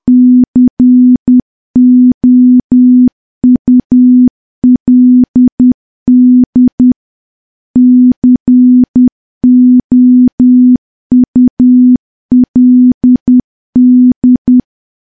Morse code, heard as a keyed tone, about 10 wpm, 250 Hz. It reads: COULD COULD
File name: morse / morse-training-10WPM